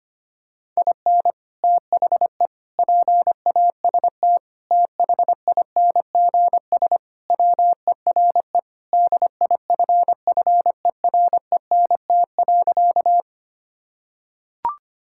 Morse code, heard as a keyed tone, about 25 wpm, 700 Hz. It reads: IN THE PAST THINGS WERE DIFFERENT.
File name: practice / 25wpm / Sentences with 3-Word Phrases - 25wpm